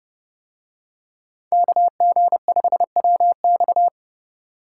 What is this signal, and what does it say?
Morse code, heard as a keyed tone, about 28 wpm, 700 Hz. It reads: KG5WX